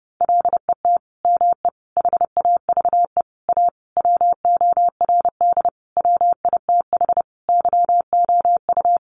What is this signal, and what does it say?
Morse code, heard as a keyed tone, about 30 wpm, 700 Hz. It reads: LET ME HAVE A WORD WITH YOU
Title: Sentences from Top 900 Words 30wpm